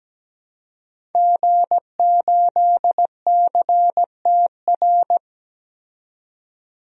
Morse code, heard as a keyed tone, about 17 wpm, 700 Hz. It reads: G8CTR
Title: DXCC-100 Call Signs - 5 Characters - 17wpm